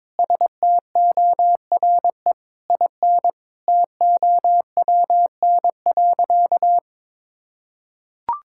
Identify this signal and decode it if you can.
Morse code, heard as a keyed tone, about 22 wpm, 700 Hz. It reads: STORE IN TOWN.
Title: Sentences from Top 1000 Words - Review Part 5 - 22wpm